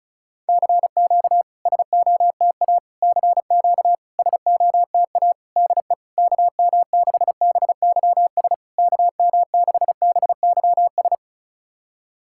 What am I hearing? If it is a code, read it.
Morse code, heard as a keyed tone, about 36 wpm, 700 Hz. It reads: CQ SOTA CQ SOTA DE KM6BYS KM6BYS
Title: SOTA QSO Practice 35wpm